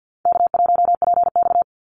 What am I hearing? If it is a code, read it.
Morse code, heard as a keyed tone, about 48 wpm, 700 Hz. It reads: K1PX
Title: Call Signs Repeated 3x - 50wpm